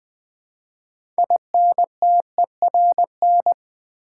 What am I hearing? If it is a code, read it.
Morse code, heard as a keyed tone, about 20 wpm, 700 Hz. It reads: INTERN
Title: Build A Word - 2 Letters At A Time - 8-Letter Words - 20wpm